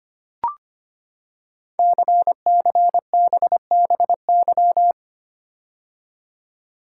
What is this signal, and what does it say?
Morse code, heard as a keyed tone, about 25 wpm, 700 Hz. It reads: CCBBY
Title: Code Groups - 5 Letters - 25wpm